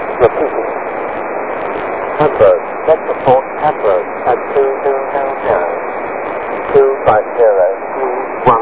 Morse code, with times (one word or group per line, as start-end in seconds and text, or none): none